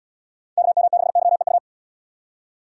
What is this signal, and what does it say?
Morse code, heard as a keyed tone, about 72 wpm, 700 Hz. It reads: KM6YEU